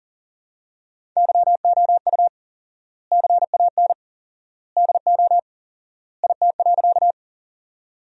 Morse code, heard as a keyed tone, about 37 wpm, 700 Hz. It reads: YOU CAN DO IT.
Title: 4x Word-Spacing - Sentences from Top 100 Words - 40wpm